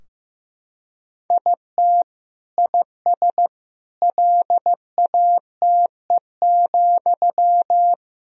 Morse code, heard as a keyed tone, about 15 wpm, 700 Hz. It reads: IT IS LATE,
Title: Sentences from Top 1000 Words - Review Part 3 - 15wpm